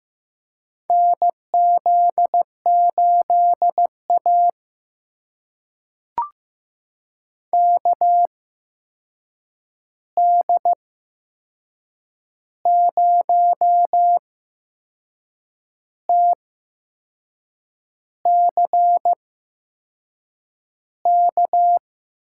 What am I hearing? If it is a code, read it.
Morse code, heard as a keyed tone, about 15 wpm, spaced as for 6 wpm, 700 Hz. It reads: NZ8A KD0TCK
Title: USA Call Signs - 8x Character Spacing - 15wpm